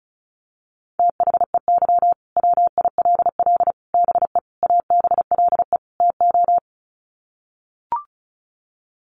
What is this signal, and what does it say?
Morse code, heard as a keyed tone, about 34 wpm, 700 Hz. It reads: THEY WILL BE ABLE TO
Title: Sets of 5 Words Encore 35wpm